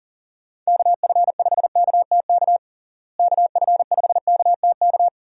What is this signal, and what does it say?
Morse code, heard as a keyed tone, about 40 wpm, 700 Hz. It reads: KF5KTK KF5KTK
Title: USA Call Signs - 6 Characters - Repeated 3x - 40wpm